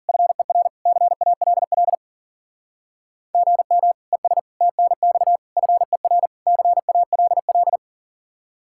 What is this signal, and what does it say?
Morse code, heard as a keyed tone, about 50 wpm, 700 Hz. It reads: FER CALL GM ES TNX FER CALL